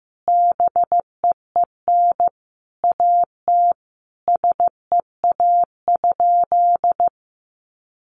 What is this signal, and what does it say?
Morse code, heard as a keyed tone, about 15 wpm, 700 Hz. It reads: BEEN AT SEA?
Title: Sentences from Top 300 Words 15wpm